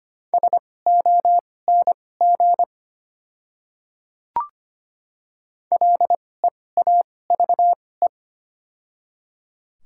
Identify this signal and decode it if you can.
Morse code, heard as a keyed tone, about 25 wpm, spaced as for 18 wpm, 690 Hz. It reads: SONG LEAVE
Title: 2x Character Spacing - Top 500 Words - 25wpm